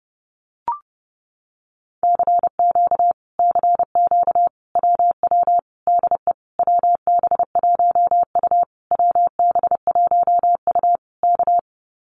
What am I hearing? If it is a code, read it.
Morse code, heard as a keyed tone, about 29 wpm, 700 Hz. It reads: CQ CQ WW DE WB1U WB1U K
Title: Common QSO Phrases Encore 30wpm